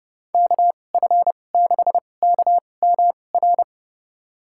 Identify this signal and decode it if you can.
Morse code, heard as a keyed tone, about 31 wpm, spaced as for 22 wpm, 700 Hz. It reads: KF6KMR
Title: USA Call Signs - 6 Characters - 2x Character Spacing - 30wpm